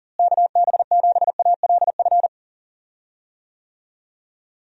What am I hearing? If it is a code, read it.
Morse code, heard as a keyed tone, about 40 wpm, 700 Hz. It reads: KB7ALF
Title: USA Call Signs - 6 Characters - Repeated 3x - 40wpm